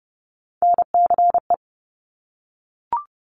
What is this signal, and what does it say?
Morse code, heard as a keyed tone, about 30 wpm, 700 Hz. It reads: NCE
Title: Top Three Letter Combinations in Words 30wpm